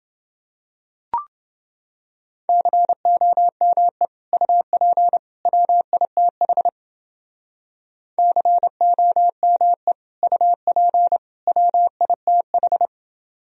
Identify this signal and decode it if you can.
Morse code, heard as a keyed tone, about 29 wpm, 700 Hz. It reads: COME UP WITH COME UP WITH